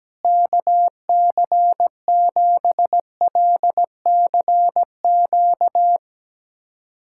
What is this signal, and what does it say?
Morse code, heard as a keyed tone, about 17 wpm, 700 Hz. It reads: KC7LCQ
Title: USA Callsign DE Callsign - 6 Characters - 17wpm